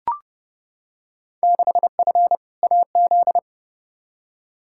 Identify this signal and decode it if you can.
Morse code, heard as a keyed tone, about 30 wpm, 700 Hz. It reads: 6F AZ
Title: Field Day Exchanges Encore - 30wpm